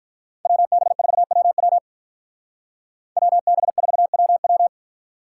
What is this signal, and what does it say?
Morse code, heard as a keyed tone, about 46 wpm, 700 Hz. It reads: WB4WW WB4WW